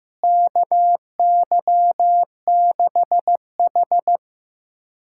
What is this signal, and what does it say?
Morse code, heard as a keyed tone, about 16 wpm, 700 Hz. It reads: KY6H
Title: USA Callsign DE Callsign - 4 Characters - 15wpm